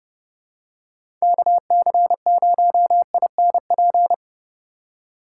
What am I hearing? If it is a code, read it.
Morse code, heard as a keyed tone, about 30 wpm, 700 Hz. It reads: KC0INP
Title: K1USN SST QSO Practice - 30wpm